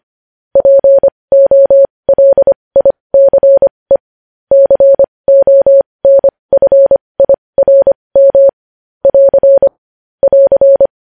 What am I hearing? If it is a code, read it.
Morse code, heard as a keyed tone, about 25 wpm, spaced as for 20 wpm, 550 Hz. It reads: POLICE CONFIRM <AR> <AR>
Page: news-headlines-in-morse-code-at-20-wpm